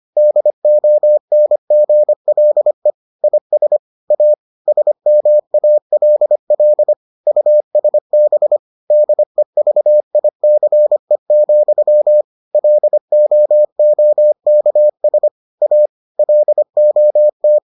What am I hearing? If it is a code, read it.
Morse code, heard as a keyed tone, about 25 wpm, 600 Hz. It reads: DONGLE IS A SMALL USB DEVICE, LOOKS A LOT